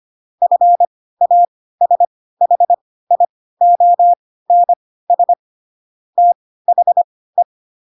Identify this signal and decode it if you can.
Morse code, heard as a keyed tone, about 25 wpm, spaced as for 16 wpm, 700 Hz. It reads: FASHIONS THE
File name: Pinocchio / Pinocchio ch3 10 25 ik1hge